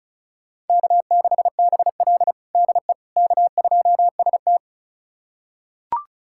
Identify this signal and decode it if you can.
Morse code, heard as a keyed tone, about 36 wpm, 700 Hz. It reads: K6BL DE K2ST
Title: USA Callsign DE Callsign - 4 Characters - 35wpm